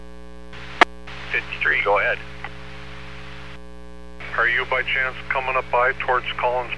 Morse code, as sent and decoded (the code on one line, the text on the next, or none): none